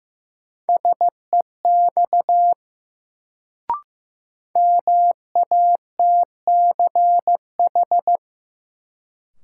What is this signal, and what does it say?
Morse code, heard as a keyed tone, about 15 wpm, 700 Hz. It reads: SEX MATCH
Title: Sentences from Top 1600 Words - New Word Drill Alt - 15wpm